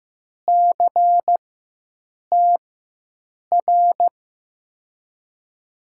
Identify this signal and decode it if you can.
Morse code, heard as a keyed tone, about 15 wpm, spaced as for 7 wpm, 700 Hz. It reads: CTR
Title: USA Call Signs - 4x Character Spacing - 15wpm